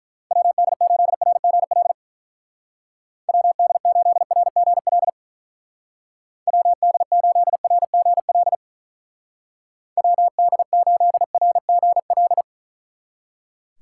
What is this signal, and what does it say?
Morse code, heard as a keyed tone, about 43 wpm, 700 Hz. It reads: WD8RGL WD8RGL WD8RGL WD8RGL
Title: Speed Racing - USA 2x3 Call Signs 35wpm